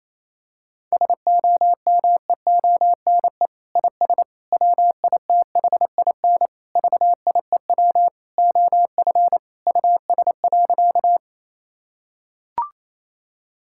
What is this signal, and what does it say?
Morse code, heard as a keyed tone, about 28 wpm, 700 Hz. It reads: SOMEONE IS WITHIN VIEW OF US.
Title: Sentences from Top 1000 Words - Review Part 6 - 28wpm